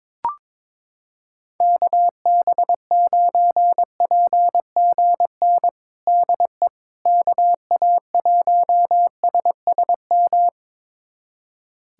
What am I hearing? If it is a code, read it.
Morse code, heard as a keyed tone, about 22 wpm, 700 Hz. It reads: KB9PGN DE KA1SSM
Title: USA Callsign DE Callsign - 6 Characters - 22wpm